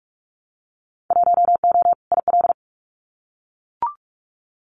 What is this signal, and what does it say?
Morse code, heard as a keyed tone, about 44 wpm, 700 Hz. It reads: 1O IL
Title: Winter Field Day Exchanges 45wpm